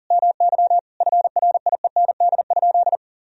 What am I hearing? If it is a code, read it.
Morse code, heard as a keyed tone, about 40 wpm, 700 Hz. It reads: MY FRIEND?
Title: Sentences from Top 1000 Words - Review Part 3 - 40wpm